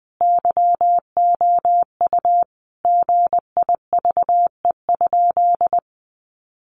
Morse code, heard as a keyed tone, about 19 wpm, 700 Hz. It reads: YOU GIVE?
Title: Sentences from Top 200 Words 20wpm